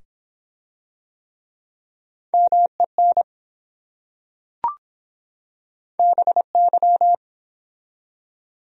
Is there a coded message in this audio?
Yes: Morse code from 2.3 to 7.2 s, heard as a keyed tone, about 24 wpm, 700 Hz. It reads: MEN BY